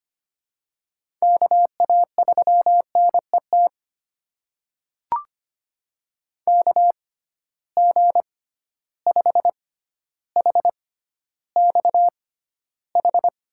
Morse code, heard as a keyed tone, about 25 wpm, spaced as for 13 wpm, 700 Hz. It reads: KA3NET KG5HXH
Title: USA Call Signs - 6 Characters - 6x Character Spacing - 25wpm